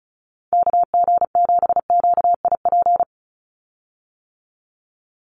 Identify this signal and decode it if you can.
Morse code, heard as a keyed tone, about 34 wpm, 700 Hz. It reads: KG7QIP
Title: USA Call Signs 35wpm